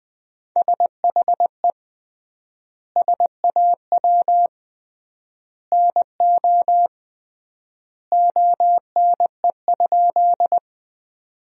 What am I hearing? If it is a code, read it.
Morse code, heard as a keyed tone, about 20 wpm, 700 Hz. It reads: SHE SAW NO ONE?
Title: 3x Word-Spacing - Sentences from Top 300 Words - 20wpm